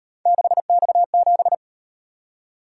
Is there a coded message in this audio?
Yes: Morse code from 0.3 to 1.5 s, heard as a keyed tone, about 38 wpm, 700 Hz. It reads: 6X7